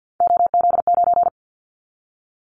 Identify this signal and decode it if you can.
Morse code, heard as a keyed tone, about 50 wpm, 700 Hz. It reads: OZ9
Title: Callsign Prefixes for Top 50 DXCC - 50wpm